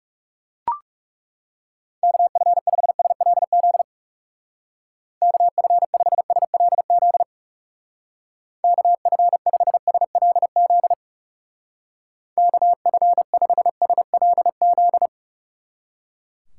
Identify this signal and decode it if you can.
Morse code, heard as a keyed tone, about 36 wpm, 700 Hz. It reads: KF5SLZ KF5SLZ KF5SLZ KF5SLZ
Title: Speed Racing - USA 2x3 Call Signs 30wpm